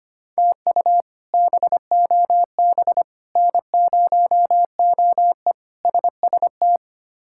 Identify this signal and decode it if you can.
Morse code, heard as a keyed tone, about 25 wpm, 700 Hz. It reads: TU BOB N0OE SST
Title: K1USN SST QSO Practice - 25wpm